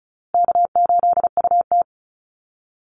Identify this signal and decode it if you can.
Morse code, heard as a keyed tone, about 34 wpm, 700 Hz. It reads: K8UT
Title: Easy Call Signs Repeated 3x - 35wpm